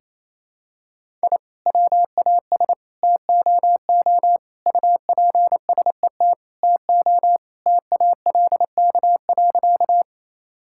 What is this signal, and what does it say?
Morse code, heard as a keyed tone, about 27 wpm, 700 Hz. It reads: I WAS TOO UPSET TO TALK.
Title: Sentences from Top 1000 Words - Review Part 5 - 28wpm